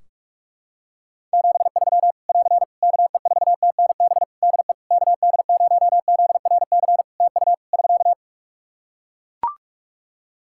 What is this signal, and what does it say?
Morse code, heard as a keyed tone, about 42 wpm, 700 Hz. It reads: 73 <AR> KE4TNB DE KD0ZRC TU <SK>
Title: K1USN SST QSO Practice 45wpm